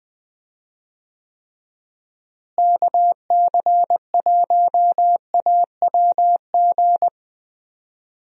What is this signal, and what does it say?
Morse code, heard as a keyed tone, about 21 wpm, 700 Hz. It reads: KC1AWG
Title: USA Call Signs - 6 Characters - Repeated 3x - 20wpm